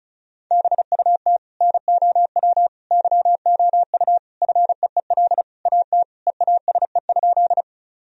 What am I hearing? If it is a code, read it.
Morse code, heard as a keyed tone, about 34 wpm, 700 Hz. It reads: BUT NOW YOU FEEL AT EASE?
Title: Sentences from Top 1000 Words - Review Part 1 - 35wpm